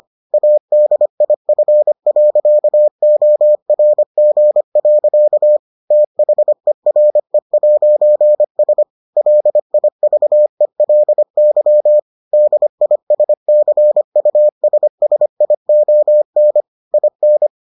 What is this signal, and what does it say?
Morse code, heard as a keyed tone, about 25 wpm, 600 Hz. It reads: ADIF.ORG. THERE'S LIVELY DISCUSSION IN